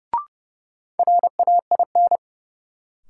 Morse code, heard as a keyed tone, about 32 wpm, 700 Hz. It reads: RAIN
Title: Sentences from Top 600 Words - New Word Drill - 30wpm